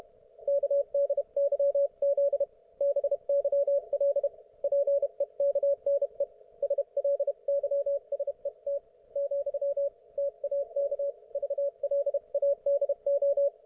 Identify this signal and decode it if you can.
Morse code, heard as a keyed tone, about 32 wpm, 560 Hz. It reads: KDYZ BYL PEKNE SLYSET , TAK VLADO